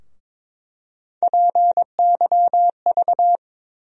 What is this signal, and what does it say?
Morse code, heard as a keyed tone, about 22 wpm, 700 Hz. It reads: PYV
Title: USA Call Sign Postfixes - 3 Letters 22wpm